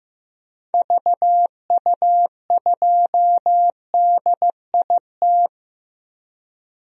Morse code, heard as a keyed tone, about 15 wpm, 700 Hz. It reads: VU2DIT
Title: DXCC-100 Call Signs - 6 Characters - Repeated 3x - 15wpm